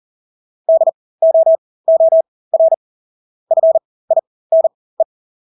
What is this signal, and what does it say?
Morse code, heard as a keyed tone, about 40 wpm, spaced as for 20 wpm, 650 Hz. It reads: DOOR FINE
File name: Engelske ord 40-20 wpm